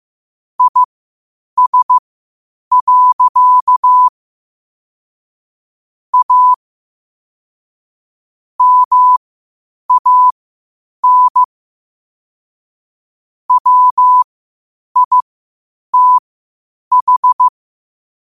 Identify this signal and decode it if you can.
Morse code, heard as a keyed tone, about 15 wpm, spaced as for 8 wpm, 1000 Hz. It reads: IS. A MAN WITH
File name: qotdfiles / 8wpm